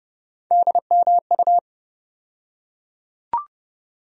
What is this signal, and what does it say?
Morse code, heard as a keyed tone, about 30 wpm, 700 Hz. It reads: DMU
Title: USA Call Sign Postfixes - 3 Letters 30wpm